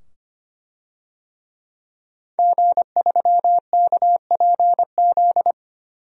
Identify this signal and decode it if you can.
Morse code, heard as a keyed tone, about 25 wpm, 700 Hz. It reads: G3KPZ
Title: DXCC-100 Call Signs - 5 Characters - 25wpm